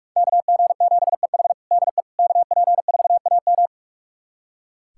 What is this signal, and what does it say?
Morse code, heard as a keyed tone, about 46 wpm, 700 Hz. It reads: KG7EH DE KP4AM
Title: USA Callsign DE Callsign - 5 Characters - 45wpm